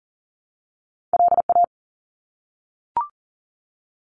Morse code, heard as a keyed tone, about 40 wpm, 700 Hz. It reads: LA
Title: ARRL Sections 40wpm